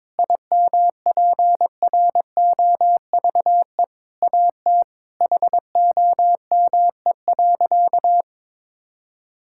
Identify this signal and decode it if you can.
Morse code, heard as a keyed tone, about 22 wpm, 700 Hz. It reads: IMPROVE AT HOME.
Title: Sentences from Top 1000 Words - Review Part 5 - 22wpm